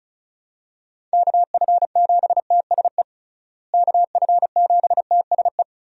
Morse code, heard as a keyed tone, about 34 wpm, 700 Hz. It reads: KF7TSE KF7TSE
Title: USA Call Signs - 6 Characters - Repeated 3x - 35wpm